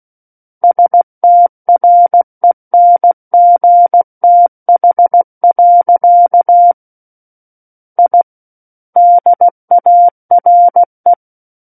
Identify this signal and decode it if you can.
Morse code, heard as a keyed tone, about 16 wpm, 700 Hz. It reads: STRENGTH. I DARE
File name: Book One / 07-Chapter 7 - 16 WPM